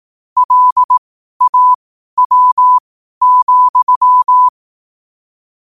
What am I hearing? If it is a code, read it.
Morse code, heard as a keyed tone, about 18 wpm, spaced as for 12 wpm, 1000 Hz. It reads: LAW,
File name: qotdfiles / 12wpm